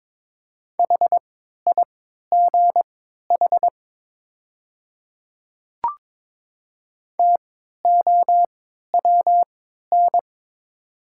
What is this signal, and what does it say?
Morse code, heard as a keyed tone, about 23 wpm, spaced as for 13 wpm, 690 Hz. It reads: HIGH TOWN
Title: 3x Character Spacing - Top 500 Words - 22wpm